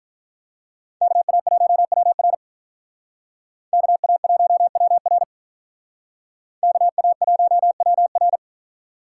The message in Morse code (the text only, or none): KA1WR KA1WR KA1WR